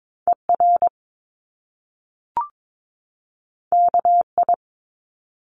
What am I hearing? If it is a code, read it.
Morse code, heard as a keyed tone, about 22 wpm, 700 Hz. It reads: ER KI
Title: Top Two Letter Combinations in Words 22wpm